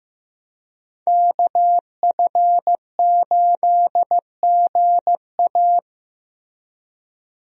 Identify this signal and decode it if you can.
Morse code, heard as a keyed tone, about 15 wpm, 700 Hz. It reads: KF8GA
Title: USA Call Signs - 5 Characters - Repeated 2x - 15wpm